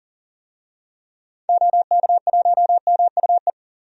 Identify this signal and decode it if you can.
Morse code, heard as a keyed tone, about 40 wpm, 700 Hz. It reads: OK1MUE
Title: DXCC-100 Call Signs - 6 Characters - 40wpm